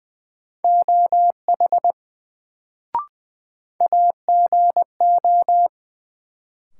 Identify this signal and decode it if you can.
Morse code, heard as a keyed tone, about 20 wpm, 700 Hz. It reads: OH AGO
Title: Sentences from Top 500 Words - New Word Drill - 20wpm